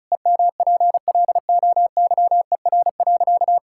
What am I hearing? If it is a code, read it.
Morse code, heard as a keyed tone, about 36 wpm, 700 Hz. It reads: EMPLOYER.
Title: Pangram Challenge v2 - 35wpm